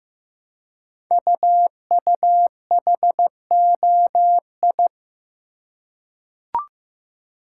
Morse code, heard as a keyed tone, about 15 wpm, 700 Hz. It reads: UUHOI
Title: Code Groups - 5 Letters - 15wpm